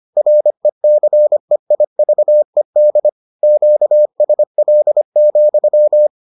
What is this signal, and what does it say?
Morse code, heard as a keyed tone, about 25 wpm, 600 Hz. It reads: RECEIVED QSL,